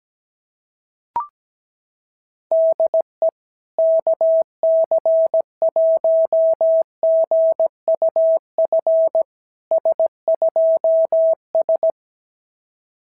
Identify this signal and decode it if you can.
Morse code, heard as a keyed tone, about 17 wpm, 650 Hz. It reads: DE KC1GUF S2S
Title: SOTA QSO Practice 17wpm